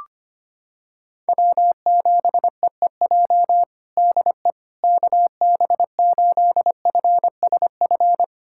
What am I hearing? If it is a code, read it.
Morse code, heard as a keyed tone, about 25 wpm, 700 Hz. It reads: W7EEJ DE KB8FSF
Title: USA Callsign DE Callsign 25wpm